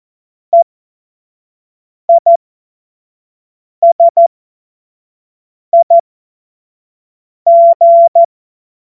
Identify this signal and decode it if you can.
Morse code, heard as a keyed tone, about 14 wpm, spaced as for 5 wpm, 680 Hz. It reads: EISIG
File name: Morsekurs / Witze / Witz 03. Mai 2022 25 F70